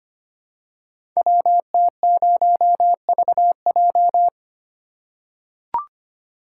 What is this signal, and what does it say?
Morse code, heard as a keyed tone, about 25 wpm, 700 Hz. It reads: WT0VJ
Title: Code Groups - 5 Letter:Numbers - 25wpm